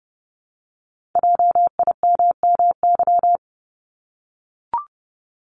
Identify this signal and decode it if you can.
Morse code, heard as a keyed tone, about 30 wpm, 700 Hz. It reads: JIMMY
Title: Multi-letter Phonemes Repeated 3x 30wpm